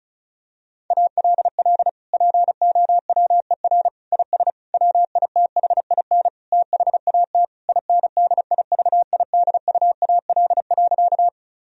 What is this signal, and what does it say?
Morse code, heard as a keyed tone, about 36 wpm, 700 Hz. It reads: ALL POWER IS WITHIN THAT INDIVIDUAL.
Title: Sentences from Top 1000 Words - Review Part 4 - 35wpm